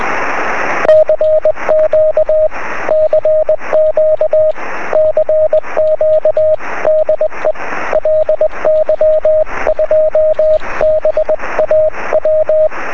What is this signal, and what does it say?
Morse code, heard as a keyed tone, about 21 wpm, 620 Hz, with radio noise. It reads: CQ CQ CQ DE LY2BAW
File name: sm7lcb / 2208 ly2baw 23cm